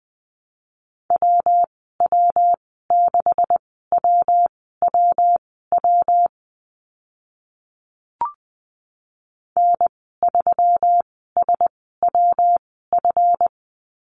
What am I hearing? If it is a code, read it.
Morse code, heard as a keyed tone, about 20 wpm, spaced as for 14 wpm, 700 Hz. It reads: WW6WWW N3SWF
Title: 140 Lesson 23g - Letter f - Call Signs 20wpm